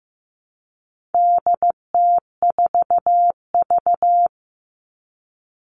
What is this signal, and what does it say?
Morse code, heard as a keyed tone, about 15 wpm, 700 Hz. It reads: DT4V
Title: All Cut Numbers - 4-Digit 15wpm